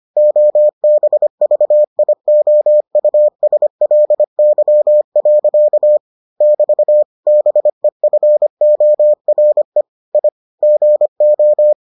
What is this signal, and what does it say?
Morse code, heard as a keyed tone, about 25 wpm, 600 Hz. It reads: OBVIOUSLY. <BT> BEFORE I GO